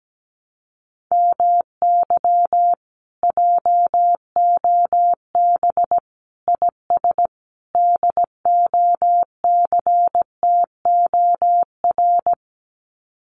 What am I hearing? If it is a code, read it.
Morse code, heard as a keyed tone, about 17 wpm, 700 Hz. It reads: MY JOB IS DOCTOR
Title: Common QSO Phrases Encore 17wpm